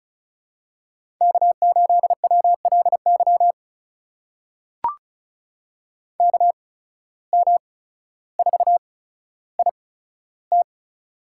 Morse code, heard as a keyed tone, about 36 wpm, spaced as for 15 wpm, 700 Hz. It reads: K8WLY KM4IT
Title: USA Call Signs - 5 Characters - 8x Character Spacing - 35wpm